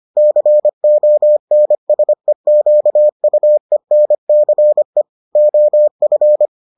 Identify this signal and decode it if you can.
Morse code, heard as a keyed tone, about 25 wpm, 600 Hz. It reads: CONSEQUENCE OF